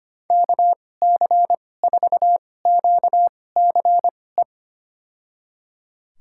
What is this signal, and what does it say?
Morse code, heard as a keyed tone, about 25 wpm, spaced as for 18 wpm, 700 Hz. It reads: KC4QCE
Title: USA Call Signs - 2x Character Spacing - 25wpm